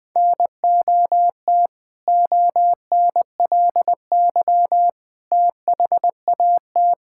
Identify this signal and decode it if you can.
Morse code, heard as a keyed tone, about 20 wpm, 700 Hz. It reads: NOT ONLY THAT
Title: Sentences with 2-Word Phrases - 20wpm